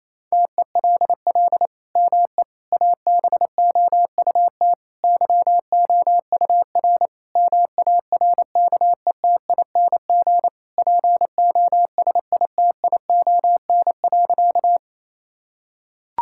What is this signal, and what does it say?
Morse code, heard as a keyed tone, about 27 wpm, 700 Hz. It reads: TELL ME ABOUT YOUR MARKETING POSITION.
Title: Sentences from Top 1000 Words - Review Part 5 - 28wpm